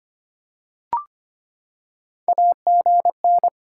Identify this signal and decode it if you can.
Morse code, heard as a keyed tone, about 25 wpm, 700 Hz. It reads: AGN
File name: practice / 25wpm / Top 20 QSO Elements - 25wpm